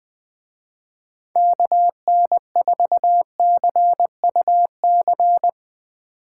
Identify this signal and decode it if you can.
Morse code, heard as a keyed tone, about 20 wpm, 700 Hz. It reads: KN4CUC